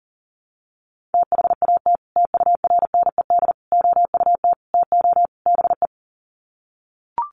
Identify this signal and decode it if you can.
Morse code, heard as a keyed tone, about 40 wpm, 700 Hz. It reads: THAT TURNED OUT TO BE
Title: Sets of 5 Words Encore2 40wpm